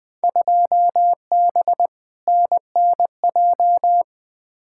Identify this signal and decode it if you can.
Morse code, heard as a keyed tone, about 20 wpm, 700 Hz. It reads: 2B NNJ
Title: Field Day Exchanges Encore - 20wpm